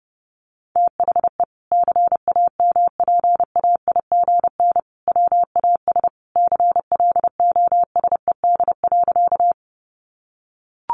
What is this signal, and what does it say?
Morse code, heard as a keyed tone, about 30 wpm, 700 Hz. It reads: THE CAMPAIGN WAS CLOSED.
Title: Sentences from Top 1400 Words 30wpm